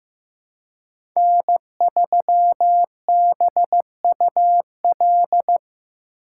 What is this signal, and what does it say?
Morse code, heard as a keyed tone, about 15 wpm, 700 Hz. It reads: N3BUL
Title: USA Call Signs - 5 Characters - 2x Character Spacing - 15wpm